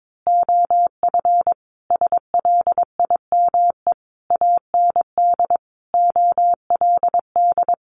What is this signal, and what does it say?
Morse code, heard as a keyed tone, about 22 wpm, 700 Hz. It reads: OF SLIME AND OLD
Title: A Sound Of Thunder 22wpm